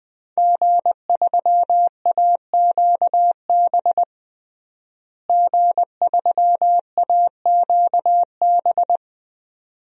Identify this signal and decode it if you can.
Morse code, heard as a keyed tone, about 20 wpm, 700 Hz. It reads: G3AQB G3AQB
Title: DXCC-100 Call Signs - 5 Characters - Repeated 2x - 20wpm